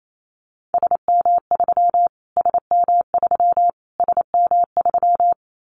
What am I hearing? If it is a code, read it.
Morse code, heard as a keyed tone, about 28 wpm, 700 Hz. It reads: SM3 SM3 SM3
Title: Callsign Prefixes for Top 25 DXCC - Repeated 3x - 28wpm